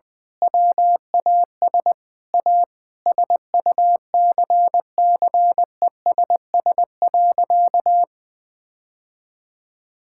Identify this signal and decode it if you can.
Morse code, heard as a keyed tone, about 21 wpm, 700 Hz. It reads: WAS A SUCCESS.
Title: Sentences with 9-Word Phrases - Spoken Beginning - 20wpm